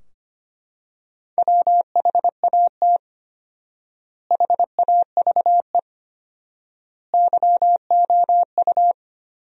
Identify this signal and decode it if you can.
Morse code, heard as a keyed tone, about 25 wpm, 700 Hz. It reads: WHAT HAVE YOU
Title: 4x Word-Spacing - Sentences from Top 100 Words - 25wpm